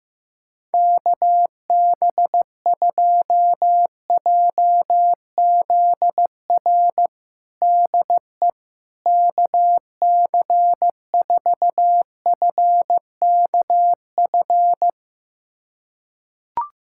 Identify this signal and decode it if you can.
Morse code, heard as a keyed tone, about 15 wpm, 700 Hz. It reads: KB2JZR DE KC4FKF
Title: USA Callsign DE Callsign - 6 Characters - 15wpm